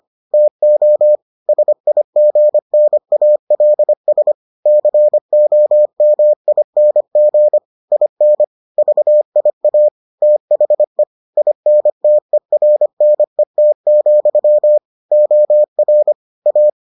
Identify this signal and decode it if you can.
Morse code, heard as a keyed tone, about 25 wpm, 600 Hz. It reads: TO SIGNALS COMING IN VIA THE INTERNET, OR A